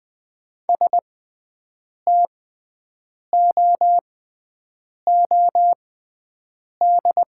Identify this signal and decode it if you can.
Morse code, heard as a keyed tone, about 20 wpm, spaced as for 7 wpm, 700 Hz. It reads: STOOD
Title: 6x Character Spacing - Top 500 Words - 20wpm